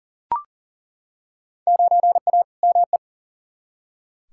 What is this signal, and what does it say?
Morse code, heard as a keyed tone, about 37 wpm, 700 Hz. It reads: 9A ME
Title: Field Day Exchanges Encore - 40wpm